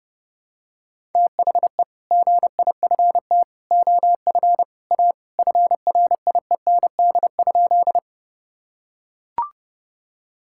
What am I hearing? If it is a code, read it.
Morse code, heard as a keyed tone, about 28 wpm, 700 Hz. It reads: THE GIFT OF A FRIEND?
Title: Sentences from Top 1000 Words - Review Part 6 - 30wpm